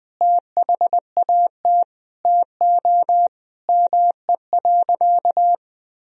Morse code, heard as a keyed tone, about 21 wpm, 700 Hz. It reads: THAT TO ME.
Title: Sentences from Top 400 Words and Abbreviations 20wpm